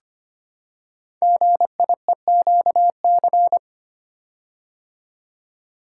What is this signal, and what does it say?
Morse code, heard as a keyed tone, about 25 wpm, 700 Hz. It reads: GIEQC
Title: Code Groups - 5 Letter:Numbers - 25wpm